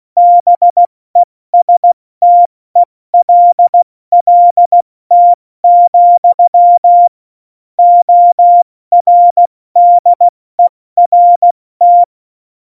Morse code, heard as a keyed tone, about 16 wpm, 700 Hz. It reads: BESTELLT, ORDERT
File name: Morsekurs / Witze / Witz 08. Oktober 2019, 70 F80